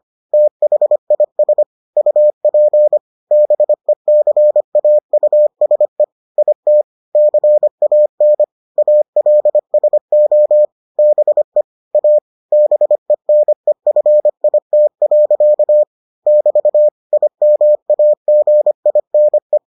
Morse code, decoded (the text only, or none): THIS UP BECAUSE IT CAN ALSO BE A BENEFIT. <BT> IMAGINE